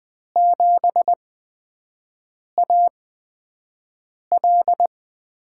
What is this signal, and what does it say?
Morse code, heard as a keyed tone, about 20 wpm, spaced as for 5 wpm, 700 Hz. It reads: 7AL
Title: USA Call Signs - 6 Characters - 8x Character Spacing - 20wpm